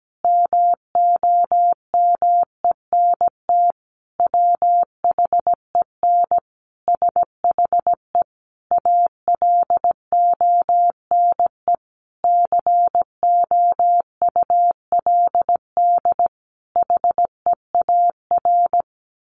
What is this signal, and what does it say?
Morse code, heard as a keyed tone, about 17 wpm, 700 Hz. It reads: MOMENT WHEN SHE ALONE COULD HEAR
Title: Sentences from Top 1000 Words 17wpm